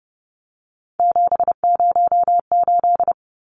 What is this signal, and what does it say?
Morse code, heard as a keyed tone, about 30 wpm, 700 Hz. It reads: 708
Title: Numbers - 3-Digit 30wpm